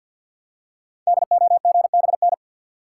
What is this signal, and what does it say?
Morse code, heard as a keyed tone, about 50 wpm, 700 Hz. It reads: DOGBN
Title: Code Groups - 5 Letters - 50wpm